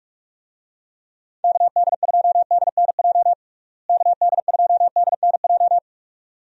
Morse code, heard as a keyed tone, about 43 wpm, 700 Hz. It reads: KD2DNJ KD2DNJ
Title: USA Call Signs - 6 Characters - Repeated 2x - 45wpm